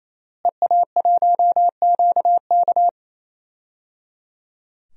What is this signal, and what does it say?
Morse code, heard as a keyed tone, about 28 wpm, 700 Hz. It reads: EA1QK
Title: DXCC-100 Call Signs - 5 Characters - Repeated 3x - 28wpm